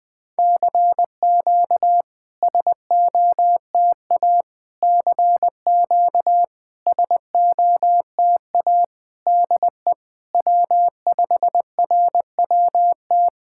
Morse code, heard as a keyed tone, about 20 wpm, 700 Hz. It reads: CQ SOTA CQ SOTA DE W5RWT
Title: SOTA QSO Practice 20wpm